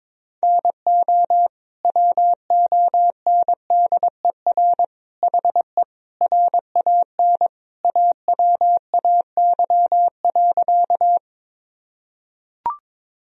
Morse code, heard as a keyed tone, about 21 wpm, 700 Hz. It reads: NO WONDER HE RAN AWAY.
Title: Sentences from Top 1000 Words - Review Part 3 - 22wpm